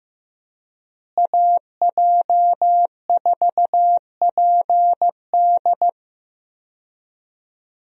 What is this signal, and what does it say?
Morse code, heard as a keyed tone, about 15 wpm, 700 Hz. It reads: AJ4PD
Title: USA Call Signs - 5 Characters - Repeated 2x - 15wpm